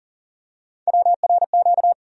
Morse code, heard as a keyed tone, about 40 wpm, 700 Hz. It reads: WRQ
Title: USA Call Sign Postfixes - 3 Letters 40wpm